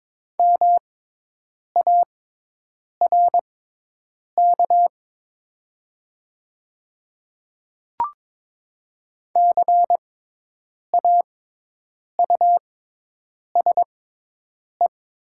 Morse code, heard as a keyed tone, about 22 wpm, spaced as for 8 wpm, 700 Hz. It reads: MARK CAUSE
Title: 6x Character Spacing - Top 300 Words - 22wpm